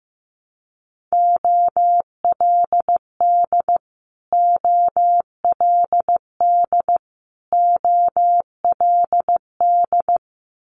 Morse code, heard as a keyed tone, about 15 wpm, 700 Hz. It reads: OLD OLD OLD
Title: Multi-letter Phonemes Repeated 3x 15wpm